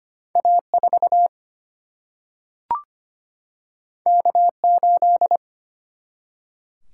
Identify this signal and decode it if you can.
Morse code, heard as a keyed tone, about 25 wpm, 700 Hz. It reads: A4 K8
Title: Single-Letter Single-Number 25wpm